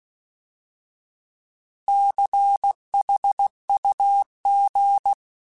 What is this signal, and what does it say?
Morse code, heard as a keyed tone, about 16 wpm, 800 Hz. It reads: CHUG
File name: Alphabetical / Alphabetical mp3s / Cs @15